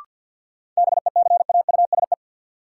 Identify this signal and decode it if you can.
Morse code, heard as a keyed tone, about 47 wpm, 700 Hz. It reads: BECAUSE
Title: Sets of 2 Words - New Word Drill - 50wpm